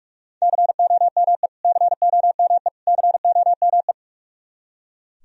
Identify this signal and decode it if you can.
Morse code, heard as a keyed tone, about 44 wpm, 700 Hz. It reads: COME COME COME
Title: Top 100 Words Repeated 45wpm